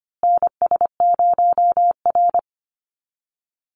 Morse code, heard as a keyed tone, about 25 wpm, 700 Hz. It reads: NS0R
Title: Call Signs Repeated 2x - 25wpm